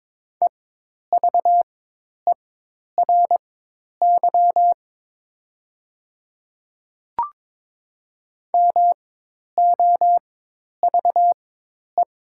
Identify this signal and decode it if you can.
Morse code, heard as a keyed tone, about 22 wpm, spaced as for 10 wpm, 700 Hz. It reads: EVERY MOVE
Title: 4x Character Spacing - Top 300 Words - 22wpm